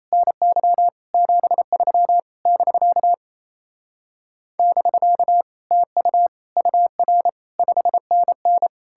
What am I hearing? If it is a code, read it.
Morse code, heard as a keyed tone, about 38 wpm, 700 Hz. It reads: NY 73 <BK> <BK> TU UR 5NN